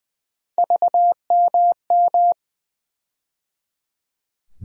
Morse code, heard as a keyed tone, about 21 wpm, 700 Hz. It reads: VMM
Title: USA Call Sign Postfixes - 3 Letters 20wpm